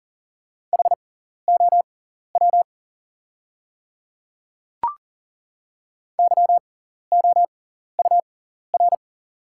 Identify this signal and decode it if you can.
Morse code, heard as a keyed tone, about 38 wpm, spaced as for 14 wpm, 690 Hz. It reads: HOW YOUR
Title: 6x Character Spacing - Top 100 Words - 40wpm